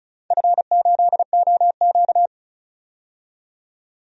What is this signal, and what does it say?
Morse code, heard as a keyed tone, about 36 wpm, 690 Hz. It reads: F8OQ
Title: DXCC-100 Call Signs - 4 Characters - 35wpm